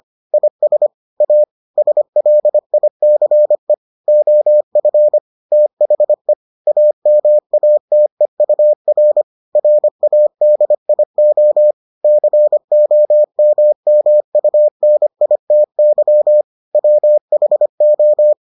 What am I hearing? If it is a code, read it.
Morse code, heard as a keyed tone, about 25 wpm, 600 Hz. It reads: IS A SLICE OF THE AMATEUR RADIO COMMUNITY WHO